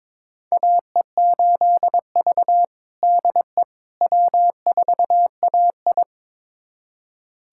Morse code, heard as a keyed tone, about 22 wpm, 700 Hz. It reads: AE8V DE W4AI
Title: USA Callsign DE Callsign - 4 Characters - 22wpm